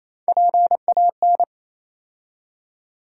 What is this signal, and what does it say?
Morse code, heard as a keyed tone, about 27 wpm, 700 Hz. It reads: PAN
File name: Easy Words - Speed-Racing with Hints - 22wpm